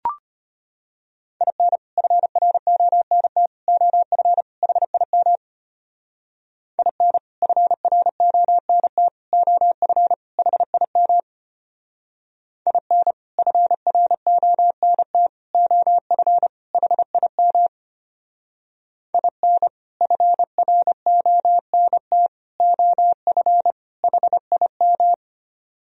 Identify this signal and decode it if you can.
Morse code, heard as a keyed tone, about 32 wpm, 700 Hz. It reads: IN FRONT OF HIM IN FRONT OF HIM IN FRONT OF HIM IN FRONT OF HIM